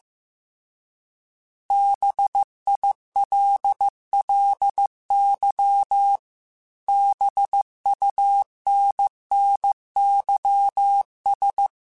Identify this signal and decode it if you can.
Morse code, heard as a keyed tone, about 15 wpm, 800 Hz. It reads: BILLY BUNNYS